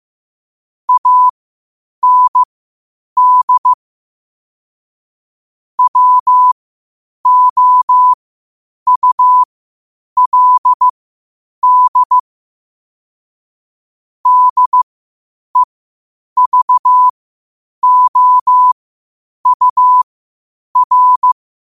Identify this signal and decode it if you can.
Morse code, heard as a keyed tone, about 15 wpm, spaced as for 8 wpm, 1000 Hz. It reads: AND WOULD DEVOUR
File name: qotdfiles / 8wpm